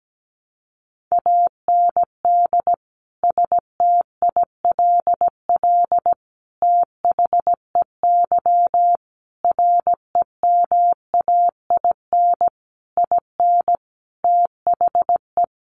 Morse code, heard as a keyed tone, about 17 wpm, 700 Hz. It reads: AND STILL THEY REMAIN IN THE
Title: Sentences from Top 1200 Words 17wpm